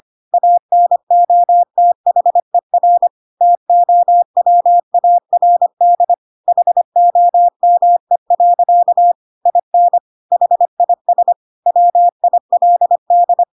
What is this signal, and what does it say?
Morse code, heard as a keyed tone, about 25 wpm, 700 Hz. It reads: ANOTHER TOWARD HOME. IN HIS WILD
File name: Pinocchio / Pinocchio ch4 25 25 ik1hge